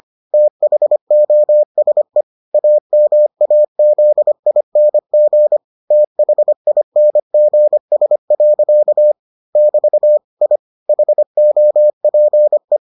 Morse code, heard as a keyed tone, about 25 wpm, 600 Hz. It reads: THOSE AMAZING THINGS. <BT> I HOPE